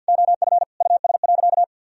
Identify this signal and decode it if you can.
Morse code, heard as a keyed tone, about 50 wpm, 700 Hz. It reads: OF US.
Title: Sentences from Top 1000 Words - Review Part 4 - 50wpm